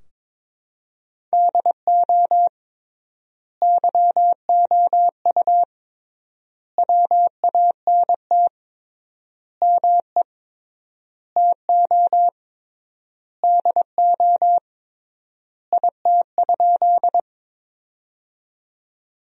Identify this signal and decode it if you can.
Morse code, heard as a keyed tone, about 22 wpm, 700 Hz. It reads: DO YOU WANT ME TO DO IT?
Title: 3x Word-Spacing - Sentences from Top 200 Words - 22wpm